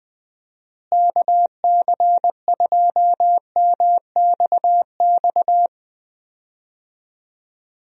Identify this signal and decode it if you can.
Morse code, heard as a keyed tone, about 20 wpm, 700 Hz. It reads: KC2MXX
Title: USA Call Signs - 6 Characters - Repeated 3x - 20wpm